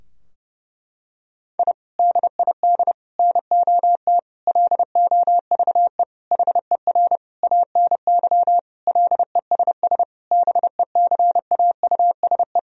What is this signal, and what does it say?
Morse code, heard as a keyed tone, about 30 wpm, 700 Hz. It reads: I DID NOT LOVE HER ANY LESS BECAUSE
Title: Sentences from Top 1000 Words - Review Part 2 - 30wpm